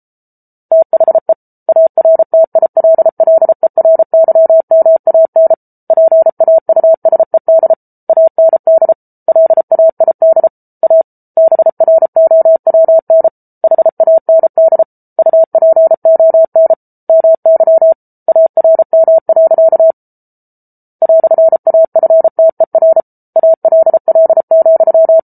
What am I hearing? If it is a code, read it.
Morse code, heard as a keyed tone, about 34 wpm, 650 Hz. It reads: THE ARTILLERYMAN PAUSED AND LAID A BROWN HAND UPON MY ARM. "AFTER ALL,